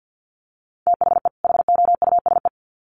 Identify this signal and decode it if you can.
Morse code, heard as a keyed tone, about 50 wpm, 700 Hz. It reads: THE HOUSE
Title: Sets of 2 Words 50wpm